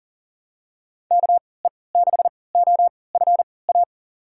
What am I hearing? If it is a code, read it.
Morse code, heard as a keyed tone, about 40 wpm, spaced as for 23 wpm, 700 Hz. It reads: KE6OFA
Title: USA Call Signs - 3x Character Spacing - 40wpm